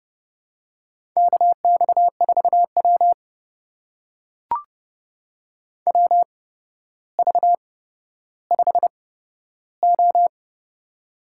Morse code, heard as a keyed tone, about 30 wpm, spaced as for 13 wpm, 700 Hz. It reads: KX4W WV5O
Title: USA Call Signs - 4 Characters - 8x Character Spacing - 30wpm